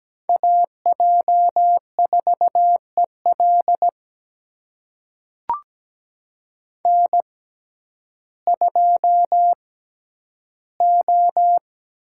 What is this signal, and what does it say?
Morse code, heard as a keyed tone, about 17 wpm, spaced as for 10 wpm, 700 Hz. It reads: AJ4EL N2O
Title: USA Call Signs - 6x Character Spacing - 17wpm